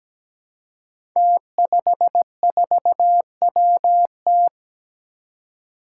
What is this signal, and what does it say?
Morse code, heard as a keyed tone, about 17 wpm, 700 Hz. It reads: T54WT
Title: Five Challenging Letter-Number Combinations 17wpm